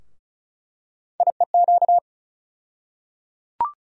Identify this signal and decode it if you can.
Morse code, heard as a keyed tone, about 34 wpm, 700 Hz. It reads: IEQ
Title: USA Call Sign Postfixes - 3 Letters 35wpm